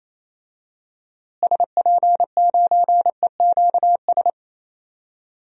Morse code, heard as a keyed tone, about 28 wpm, 700 Hz. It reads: SP9EQS
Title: DXCC-100 Call Signs - 6 Characters - Repeated 3x - 28wpm